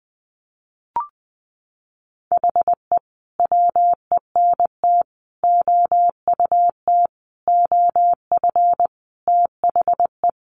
Morse code, heard as a keyed tone, about 20 wpm, 700 Hz. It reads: HE WENT OUT OF THE
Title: Sentences from Top 1100 Words 20wpm